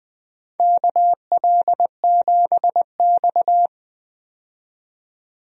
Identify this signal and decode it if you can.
Morse code, heard as a keyed tone, about 20 wpm, 700 Hz. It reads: KL7X
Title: DXCC-100 Call Signs - 4 Characters - 20wpm